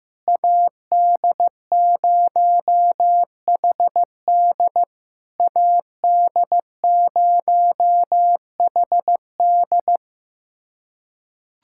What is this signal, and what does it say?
Morse code, heard as a keyed tone, about 15 wpm, 700 Hz. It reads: AD0HD AD0HD
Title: SOTA QSO Practice 15wpm